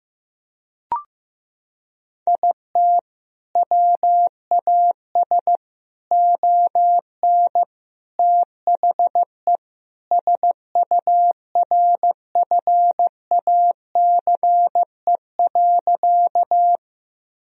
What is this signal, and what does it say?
Morse code, heard as a keyed tone, about 15 wpm, 700 Hz. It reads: IT WAS ON THE SURFACE.
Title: Sentences from Top 1000 Words - Review Part 3 - 15wpm